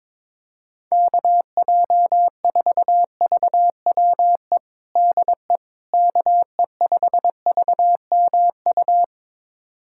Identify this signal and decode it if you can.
Morse code, heard as a keyed tone, about 22 wpm, 700 Hz. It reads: KJ4VWE DE KE5VMU
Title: USA Callsign DE Callsign - 6 Characters - 22wpm